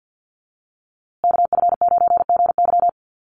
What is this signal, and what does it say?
Morse code, heard as a keyed tone, about 50 wpm, 700 Hz. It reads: KF9GY
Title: USA Call Signs Repeated 2x 50wpm